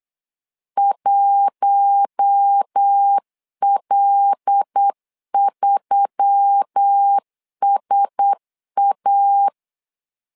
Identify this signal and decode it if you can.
Morse code, heard as a keyed tone, about 8 wpm, 800 Hz. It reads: 1L3SA
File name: LEZIONI / 5 LBVC / 5 L 8wpm 2831